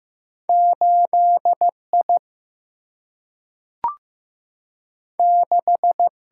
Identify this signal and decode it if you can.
Morse code, heard as a keyed tone, about 15 wpm, 700 Hz. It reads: 8I 6
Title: Single-Letter Single-Number 15wpm